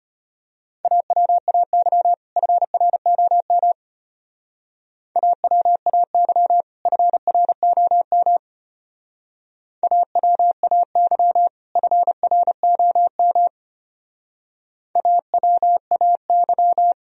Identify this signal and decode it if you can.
Morse code, heard as a keyed tone, about 32 wpm, 700 Hz. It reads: AWAY FROM AWAY FROM AWAY FROM AWAY